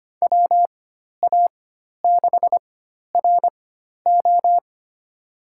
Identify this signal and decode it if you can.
Morse code, heard as a keyed tone, about 25 wpm, spaced as for 12 wpm, 700 Hz. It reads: WA6RO
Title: USA Call Signs - 5 Characters - 4x Character Spacing - 25wpm